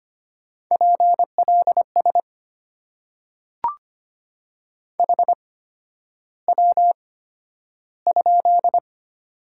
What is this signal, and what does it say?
Morse code, heard as a keyed tone, about 25 wpm, spaced as for 11 wpm, 700 Hz. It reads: PLS HW?
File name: Top 90 QSO Elements - 8x Character Spacing - 25wpm